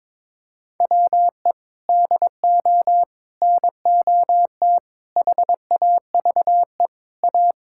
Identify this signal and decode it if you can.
Morse code, heard as a keyed tone, about 22 wpm, 700 Hz. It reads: WE DO NOT HAVE A
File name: practice / 22wpm / Sentences with 3-Word Phrases - 22wpm